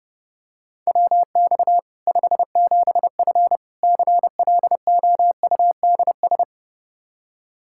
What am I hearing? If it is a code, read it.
Morse code, heard as a keyed tone, about 30 wpm, 700 Hz. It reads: WX 57F CLOUDS
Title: K1USN SST QSO Practice 30wpm